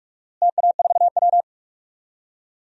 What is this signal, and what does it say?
Morse code, heard as a keyed tone, about 46 wpm, 690 Hz. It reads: TA4W
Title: DXCC-100 Call Signs - 4 Characters - Repeated 3x - 45wpm